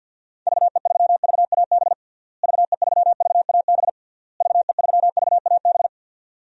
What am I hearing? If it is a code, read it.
Morse code, heard as a keyed tone, about 47 wpm, 700 Hz. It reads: VE3VAB VE3VAB VE3VAB
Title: DXCC-100 Call Signs - 6 Characters - Repeated 3x - 50wpm